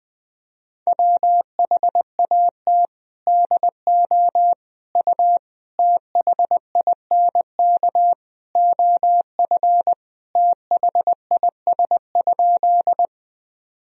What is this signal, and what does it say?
Morse code, heard as a keyed tone, about 21 wpm, 700 Hz. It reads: WHAT DO U THINK OF THIS?
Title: Sentences from Top 200 Words and Abbreviations 20wpm